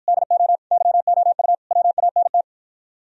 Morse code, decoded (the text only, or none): DO YOU WANT